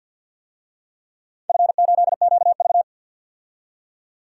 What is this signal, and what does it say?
Morse code, heard as a keyed tone, about 53 wpm, 700 Hz. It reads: F8QV
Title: DXCC-100 Call Signs - 4 Characters - 50wpm